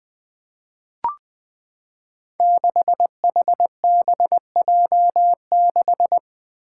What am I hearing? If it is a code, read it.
Morse code, heard as a keyed tone, about 20 wpm, 700 Hz. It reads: 6HBJ6
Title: Five Challenging Letter-Number Combinations 20wpm